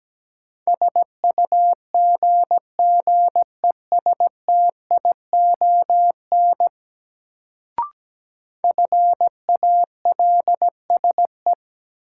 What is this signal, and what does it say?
Morse code, heard as a keyed tone, about 17 wpm, 700 Hz. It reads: SUGGESTION FALSE
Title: Sentences from Top 1600 Words - New Word Drill Alt - 17wpm